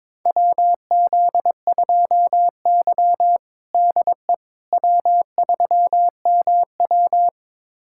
Morse code, heard as a keyed tone, about 22 wpm, 700 Hz. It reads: WZ2Y DE W3MW
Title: USA Callsign DE Callsign - 4 Characters - 22wpm